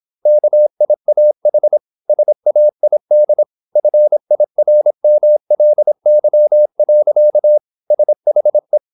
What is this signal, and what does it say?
Morse code, heard as a keyed tone, about 26 wpm, 600 Hz. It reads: KIAH SAID FIRMLY. SHE
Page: Never 26wpm | CW med Gnister